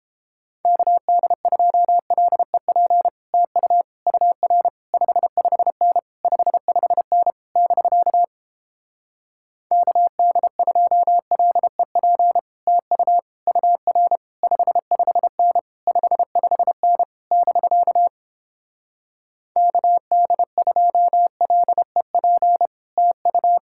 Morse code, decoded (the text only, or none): KD2LEP TU UR 55N 55N <BK> KD2LEP TU UR 55N 55N <BK> KD2LEP TU